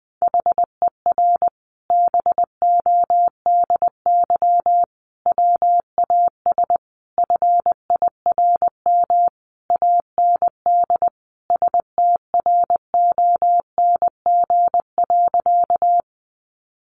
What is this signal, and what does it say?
Morse code, heard as a keyed tone, about 20 wpm, 700 Hz. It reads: HER BODY WAS FIRM AND STRONG.
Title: Sentences from Top 1400 Words 20wpm